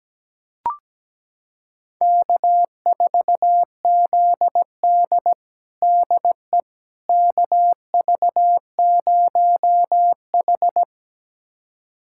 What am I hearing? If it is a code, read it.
Morse code, heard as a keyed tone, about 17 wpm, 700 Hz. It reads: K4ZD DE KV0H
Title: USA Callsign DE Callsign - 4 Characters - 17wpm